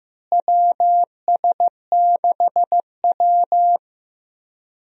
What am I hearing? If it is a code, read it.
Morse code, heard as a keyed tone, about 15 wpm, 700 Hz. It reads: WS6W
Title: USA Call Signs - 4 Characters - Repeated 3x - 15wpm